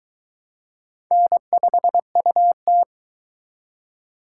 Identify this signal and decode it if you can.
Morse code, heard as a keyed tone, about 23 wpm, 700 Hz. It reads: N5UT